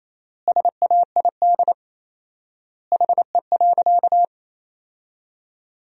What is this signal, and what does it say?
Morse code, heard as a keyed tone, about 28 wpm, 700 Hz. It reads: SAID HE.
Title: 4x Word-Spacing - Sentences from Top 100 Words - 28wpm